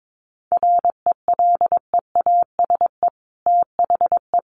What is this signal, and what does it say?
Morse code, heard as a keyed tone, about 22 wpm, 700 Hz. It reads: RELEASE THE
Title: Sentences from Top 1100 Words 22wpm